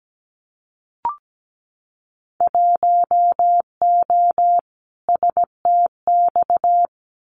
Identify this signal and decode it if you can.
Morse code, heard as a keyed tone, about 17 wpm, 700 Hz. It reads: 1O STX
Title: Winter Field Day Exchanges 17wpm